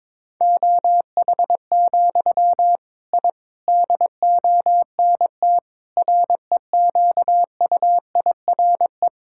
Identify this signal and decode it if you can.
Morse code, heard as a keyed tone, about 22 wpm, 700 Hz. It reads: OH, I DONT REQUIRE
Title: Sentences from Top 1000 Words - Review Part 6 - 22wpm